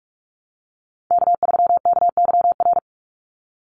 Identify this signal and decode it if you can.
Morse code, heard as a keyed tone, about 44 wpm, 700 Hz. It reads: K3KYR
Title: NA SOTA Chaser Callsigns 45wpm